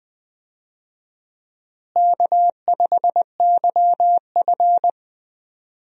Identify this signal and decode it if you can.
Morse code, heard as a keyed tone, about 21 wpm, 700 Hz. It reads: K5YF
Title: USA Call Signs - 4 Characters - Repeated 2x - 20wpm